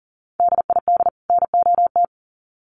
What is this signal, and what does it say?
Morse code, heard as a keyed tone, about 40 wpm, 700 Hz. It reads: DID NOT
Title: Sets of 2 Words 40wpm